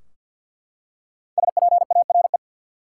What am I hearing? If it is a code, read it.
Morse code, heard as a keyed tone, about 52 wpm, 700 Hz. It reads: SPARE
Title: Five Letter Words - 50wpm